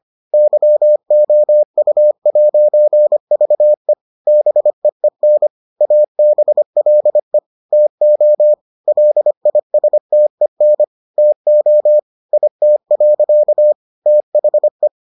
Morse code, heard as a keyed tone, about 25 wpm, 600 Hz. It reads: YOU'VE BEEN ABLE TO LISTEN TO IT. THE